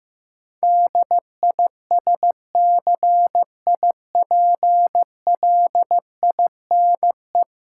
Morse code, heard as a keyed tone, about 15 wpm, 700 Hz. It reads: DISCIPLINE
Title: Sentences from Top 900 Words - New Word Drill - 15wpm